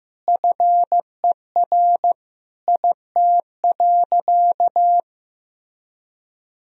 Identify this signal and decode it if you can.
Morse code, heard as a keyed tone, about 16 wpm, 700 Hz. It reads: FER IT.
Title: Sentences from Top 300 Words and Abbreviations 15wpm